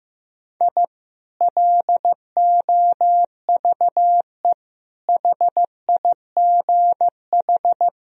Morse code, heard as a keyed tone, about 15 wpm, 700 Hz. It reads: I LOVE HIGH
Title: Sentences from Top 1000 Words - Review Part 4 - 15wpm